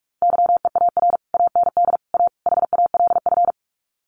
Morse code, heard as a keyed tone, about 44 wpm, 700 Hz. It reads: YEAR AND A HALF
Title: Sets of 4 Words 45wpm